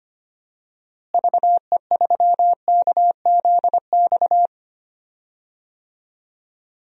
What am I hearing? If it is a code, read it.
Morse code, heard as a keyed tone, about 25 wpm, 700 Hz. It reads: VE3KZX
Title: DXCC-100 Call Signs - 6 Characters - Repeated 3x - 25wpm